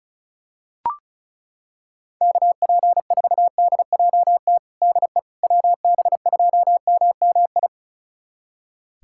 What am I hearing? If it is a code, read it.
Morse code, heard as a keyed tone, about 36 wpm, 700 Hz. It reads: KP4DJT DE WB2MMI
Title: USA Callsign DE Callsign - 6 Characters - 35wpm